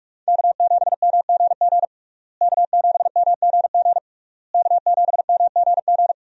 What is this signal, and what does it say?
Morse code, heard as a keyed tone, about 46 wpm, 700 Hz. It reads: K7MGG K7MGG K7MGG
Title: USA Call Signs - 5 Characters - Repeated 3x - 45wpm